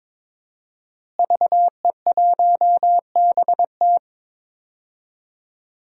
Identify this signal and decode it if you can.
Morse code, heard as a keyed tone, about 22 wpm, 700 Hz. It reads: VE1BT
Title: DXCC-100 Call Signs - 5 Characters - Repeated 2x - 22wpm